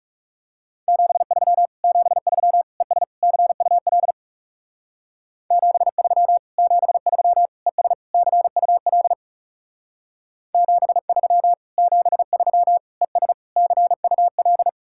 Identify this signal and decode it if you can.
Morse code, heard as a keyed tone, about 35 wpm, 700 Hz. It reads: 73 73 ES CUL 73 73 ES CUL 73 73 ES CUL
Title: Speed-Racing Common QSO Phrases Encore - 30wpm